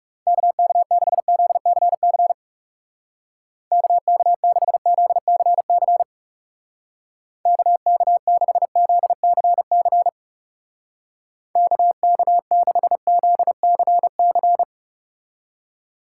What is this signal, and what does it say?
Morse code, heard as a keyed tone, about 35 wpm, 700 Hz. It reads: KK6ZCC KK6ZCC KK6ZCC KK6ZCC
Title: Speed Racing - USA 2x3 Call Signs 30wpm